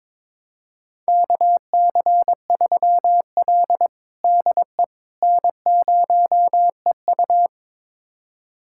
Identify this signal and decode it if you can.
Morse code, heard as a keyed tone, about 22 wpm, 700 Hz. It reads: KC3L DE N0EU
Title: USA Callsign DE Callsign - 4 Characters - 22wpm